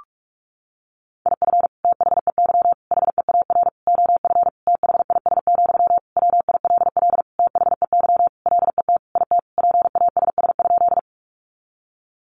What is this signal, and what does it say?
Morse code, heard as a keyed tone, about 44 wpm, 700 Hz. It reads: IF THEY HEAR OF THIS, WILL THEY LET IT PASS?
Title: Sentences from Top 400 Words 45wpm